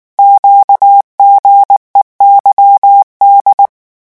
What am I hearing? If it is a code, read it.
Morse code, heard as a keyed tone, about 19 wpm, 800 Hz. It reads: QGEYD
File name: morse mp3 / revision7-vitesse 18 mots